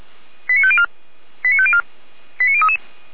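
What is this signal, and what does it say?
Five-tone selective call (ZVEI), heard as quick runs of five tones, tones of about 70 ms. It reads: 89504 89504 89033
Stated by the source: ZVEI five-tone selective call received on an SDR.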